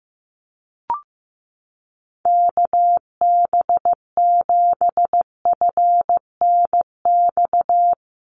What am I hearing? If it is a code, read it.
Morse code, heard as a keyed tone, about 15 wpm, 700 Hz. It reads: KB7FNX
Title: USA Call Signs Repeated 2x 15wpm